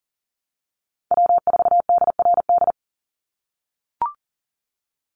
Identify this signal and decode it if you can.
Morse code, heard as a keyed tone, about 40 wpm, 700 Hz. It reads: W4DRD
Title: USA Call Signs 40wpm